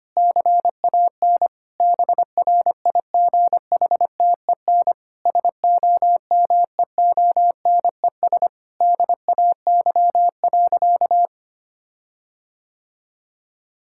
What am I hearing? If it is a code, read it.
Morse code, heard as a keyed tone, about 25 wpm, 700 Hz. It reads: CAN BRIGHTEN SOMEONES DAY.
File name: practice / 25wpm / Difficult Sentences with Hints - 25wpm